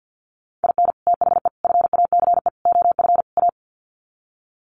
Morse code, heard as a keyed tone, about 50 wpm, 700 Hz. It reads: IN THE FACE OF A
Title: Sets of 5 Words Encore 50wpm